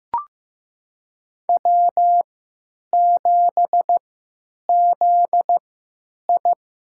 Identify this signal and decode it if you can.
Morse code, heard as a keyed tone, about 15 wpm, spaced as for 9 wpm, 700 Hz. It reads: W7ZI
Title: USA Call Signs - 4 Characters - 3x Character Spacing - 15wpm